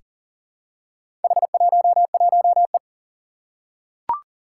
Morse code, heard as a keyed tone, about 40 wpm, 700 Hz. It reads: H11E
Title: Four Challenging Letter-Number Combinations 40wpm